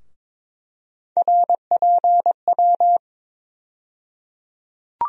Morse code, heard as a keyed tone, about 22 wpm, 700 Hz. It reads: RPW
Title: USA Call Sign Postfixes - 3 Letters 22wpm